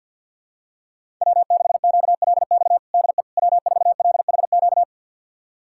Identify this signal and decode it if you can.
Morse code, heard as a keyed tone, about 48 wpm, 700 Hz. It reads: W6QLX DE W4LHQ
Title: USA Callsign DE Callsign - 5 Characters - 50wpm